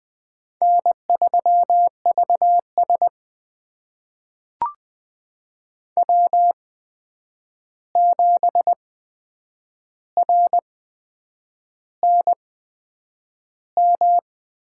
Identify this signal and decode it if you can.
Morse code, heard as a keyed tone, about 20 wpm, spaced as for 8 wpm, 700 Hz. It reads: N3VS W7RNM
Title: USA Call Signs - 8x Character Spacing - 20wpm